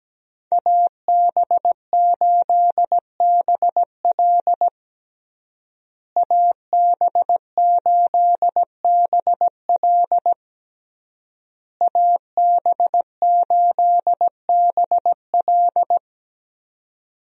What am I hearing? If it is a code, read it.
Morse code, heard as a keyed tone, about 17 wpm, 700 Hz. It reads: AB8BL AB8BL AB8BL
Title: USA Call Signs - 5 Characters - Repeated 3x - 17wpm